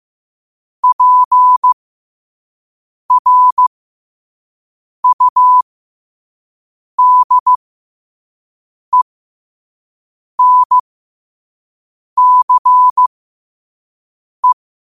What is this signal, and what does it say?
Morse code, heard as a keyed tone, about 15 wpm, spaced as for 5 wpm, 1000 Hz. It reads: PRUDENCE
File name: qotdfiles / 5wpm